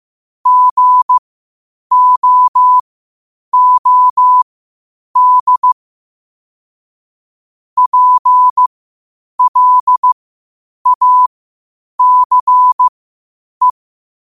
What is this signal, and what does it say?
Morse code, heard as a keyed tone, about 15 wpm, spaced as for 8 wpm, 1000 Hz. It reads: GOOD PLACE
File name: qotdfiles / 8wpm